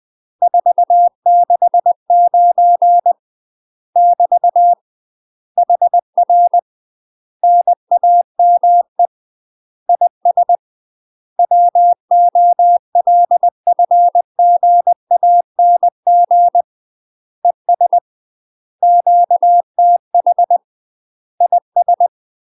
Morse code, heard as a keyed tone, about 20 wpm, 700 Hz. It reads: 469 <BT> HR NAME IS WOLFGANG ES QTH IS
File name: UK MASTER QSOs TRAINING / 20WPM / UK MASTER QSOs 0055